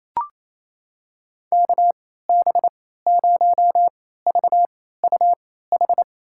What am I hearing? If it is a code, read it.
Morse code, heard as a keyed tone, about 28 wpm, spaced as for 16 wpm, 700 Hz. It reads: KB0VUH
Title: USA Call Signs - 6 Characters - 3x Character Spacing - 28wpm